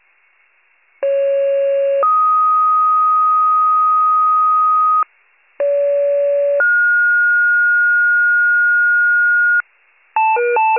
Two-tone page, heard as a one-second tone, then a three-second tone, about 550 then 1250 Hz, about 570 then 1430 Hz, with noise. It is heard from 1.0 to 9.6 s.